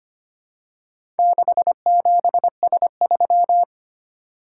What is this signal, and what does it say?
Morse code, heard as a keyed tone, about 25 wpm, 700 Hz. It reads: 67S3
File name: practice / 25wpm / Four Letter-Number Combinations - 25wpm